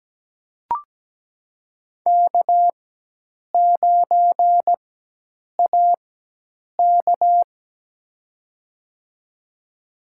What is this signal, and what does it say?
Morse code, heard as a keyed tone, about 17 wpm, spaced as for 8 wpm, 700 Hz. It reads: K9AK
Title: USA Call Signs - 4x Character Spacing - 17wpm